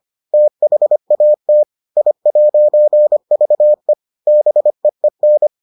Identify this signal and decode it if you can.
Morse code, heard as a keyed tone, about 25 wpm, 600 Hz. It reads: THAT I'VE BEEN